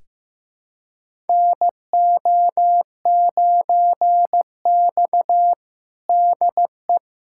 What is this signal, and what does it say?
Morse code, heard as a keyed tone, about 15 wpm, 700 Hz. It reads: NO9X DE
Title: USA Callsign DE Callsign - 4 Characters - 15wpm